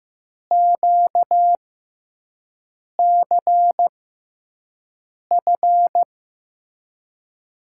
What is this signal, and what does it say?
Morse code, heard as a keyed tone, about 15 wpm, spaced as for 5 wpm, 700 Hz. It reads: QCF
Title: USA Call Signs - 6x Character Spacing - 15wpm